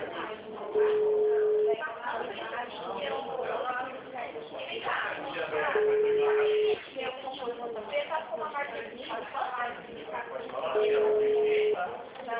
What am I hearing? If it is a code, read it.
Ringback tone (Europe, 425 Hz), with speech.
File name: out- -2050- - - .56772